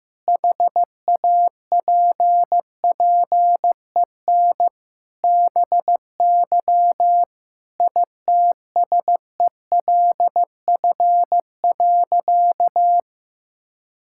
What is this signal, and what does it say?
Morse code, heard as a keyed tone, about 15 wpm, 700 Hz. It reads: HAPPEN BY ITSELF.
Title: Sentences from Top 1000 Words - Review Part 5 - 15wpm